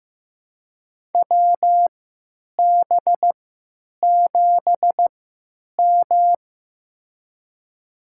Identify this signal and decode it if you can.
Morse code, heard as a keyed tone, about 15 wpm, spaced as for 9 wpm, 700 Hz. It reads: WB7M
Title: USA Call Signs - 4 Characters - 3x Character Spacing - 15wpm